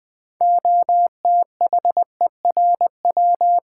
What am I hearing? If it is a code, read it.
Morse code, heard as a keyed tone, about 20 wpm, 700 Hz. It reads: OTHERW
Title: Build A Word - 3 Letters At A Time - 9-Letter Words - 20wpm